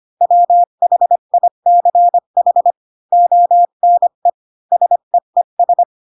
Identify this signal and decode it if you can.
Morse code, heard as a keyed tone, about 25 wpm, 700 Hz. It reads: WHICH ONE SEES
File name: Pinocchio / Pinocchio ch4 20 25 ik1hge